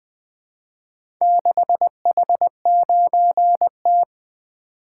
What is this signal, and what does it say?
Morse code, heard as a keyed tone, about 20 wpm, 700 Hz. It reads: 6H9T
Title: Four Letter-Number Combinations - 20wpm